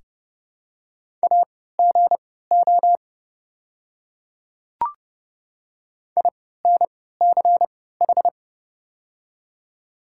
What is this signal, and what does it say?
Morse code, heard as a keyed tone, about 30 wpm, spaced as for 17 wpm, 700 Hz. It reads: AGO INCH
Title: 3x Character Spacing - Top 500 Words - 30wpm